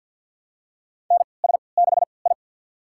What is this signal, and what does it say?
Morse code, heard as a keyed tone, about 53 wpm, spaced as for 29 wpm, 700 Hz. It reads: NS6I
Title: USA Call Signs - 4 Characters - 3x Character Spacing - 50wpm